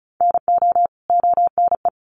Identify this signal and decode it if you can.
Morse code, heard as a keyed tone, about 34 wpm, 700 Hz. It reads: NO ONE
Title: Sets of 2 Words 35wpm